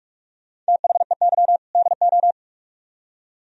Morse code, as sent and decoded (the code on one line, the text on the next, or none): - .... -.-- / -.. ---
THEY DO